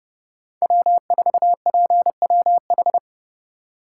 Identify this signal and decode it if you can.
Morse code, heard as a keyed tone, about 28 wpm, 700 Hz. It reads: W4PWH